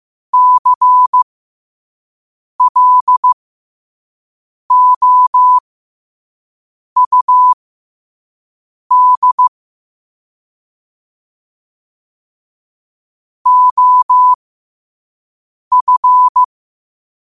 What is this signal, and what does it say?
Morse code, heard as a keyed tone, about 15 wpm, spaced as for 5 wpm, 1000 Hz. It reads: CLOUD OF
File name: qotdfiles / 5wpm